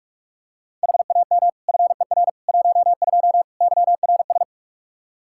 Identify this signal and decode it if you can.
Morse code, heard as a keyed tone, about 41 wpm, 700 Hz. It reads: HAM FER 12 YRS
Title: Practice QSO - Exchange 3 - More Info - 45wpm